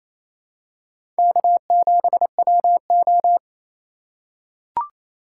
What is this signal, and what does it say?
Morse code, heard as a keyed tone, about 27 wpm, 700 Hz. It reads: K7WO
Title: USA Call Signs - 4 Characters - 4x Character Spacing - 28wpm